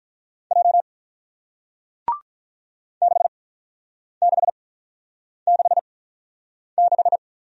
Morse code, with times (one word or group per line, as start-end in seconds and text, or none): 0.5-0.8 J
3.0-7.2 6666